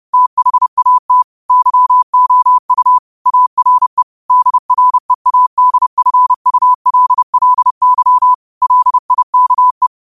Morse code, heard as a keyed tone, about 30 wpm, 1000 Hz. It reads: THAT YOU ARE DREADFULLY LIKE
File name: qotdfiles / 30wpm